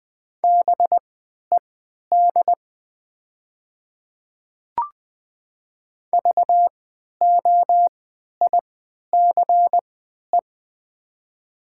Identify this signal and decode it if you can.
Morse code, heard as a keyed tone, about 20 wpm, spaced as for 11 wpm, 700 Hz. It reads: BED VOICE
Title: 3x Character Spacing - Top 500 Words - 20wpm